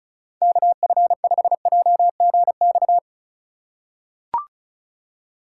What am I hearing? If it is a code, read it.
Morse code, heard as a keyed tone, about 36 wpm, 700 Hz. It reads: KF5JGX
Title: USA Call Signs - 2x Character Spacing - 35wpm